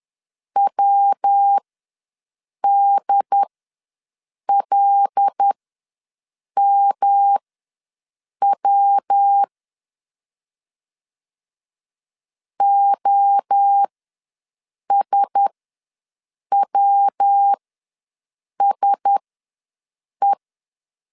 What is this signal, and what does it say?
Morse code, heard as a keyed tone, about 10 wpm, spaced as for 6 wpm, 800 Hz. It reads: WDLMW OSWSE
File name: LEZIONI / 5 LBVC / 5 LB ws 10wpm 3616